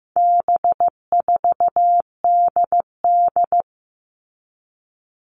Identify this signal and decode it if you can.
Morse code, heard as a keyed tone, about 15 wpm, 700 Hz. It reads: B4DD
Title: All Cut Numbers - 4-Digit 15wpm